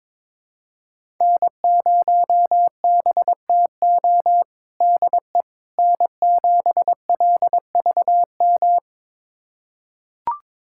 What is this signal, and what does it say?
Morse code, heard as a keyed tone, about 21 wpm, 700 Hz. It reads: N0BTO DE N7LVM
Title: USA Callsign DE Callsign - 5 Characters - 22wpm